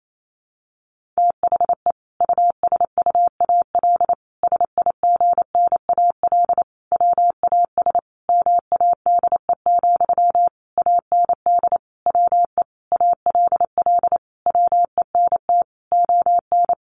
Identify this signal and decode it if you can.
Morse code, heard as a keyed tone, about 28 wpm, 700 Hz. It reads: THE USUAL SIGNAL WAS MADE, AND WE ALL WENT ON
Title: Sentences from Top 1400 Words 28wpm